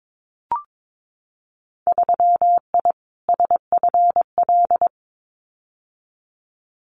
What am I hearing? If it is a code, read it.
Morse code, heard as a keyed tone, about 22 wpm, 700 Hz. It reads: 3I SFL
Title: Winter Field Day Exchanges 22wpm